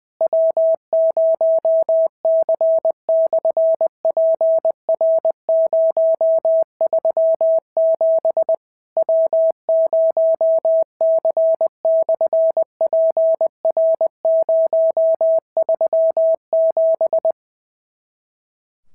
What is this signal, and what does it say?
Morse code, heard as a keyed tone, about 20 wpm, 650 Hz. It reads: W0C/PR037 W0C/PR037
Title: SOTA QSO Practice 20wpm